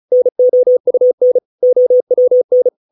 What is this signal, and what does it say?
Morse code, heard as a keyed tone, about 34 wpm, 500 Hz. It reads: NOUN OWN